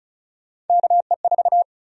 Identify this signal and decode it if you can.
Morse code, heard as a keyed tone, about 36 wpm, 700 Hz. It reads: KE4
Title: USA Call Sign Prefixes - Repeated 2x - 35wpm